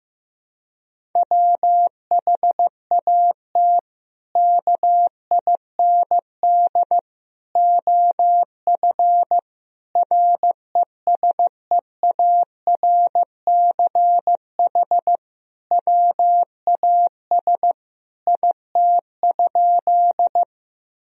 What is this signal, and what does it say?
Morse code, heard as a keyed tone, about 16 wpm, 700 Hz. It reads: WHAT KIND OF RESEARCH WAS IT?
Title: Sentences from Top 1000 Words - Review Part 4 - 15wpm